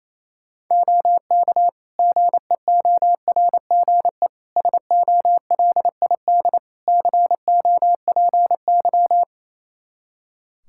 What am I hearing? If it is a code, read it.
Morse code, heard as a keyed tone, about 28 wpm, 700 Hz. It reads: OK GEORGE SOLID COPY
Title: K1USN SST QSO Practice 28wpm